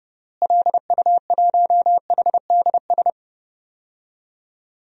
Morse code, heard as a keyed tone, about 32 wpm, 700 Hz. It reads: LU1HDS
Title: DXCC-100 Call Signs - 6 Characters - Repeated 3x - 30wpm